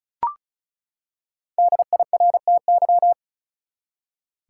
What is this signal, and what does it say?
Morse code, heard as a keyed tone, about 36 wpm, 700 Hz. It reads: DIRTY
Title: Five Letter Words - 35wpm